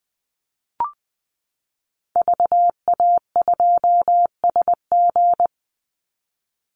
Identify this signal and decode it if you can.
Morse code, heard as a keyed tone, about 20 wpm, 700 Hz. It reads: VA2SG
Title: Call Signs Repeated 2x - 20wpm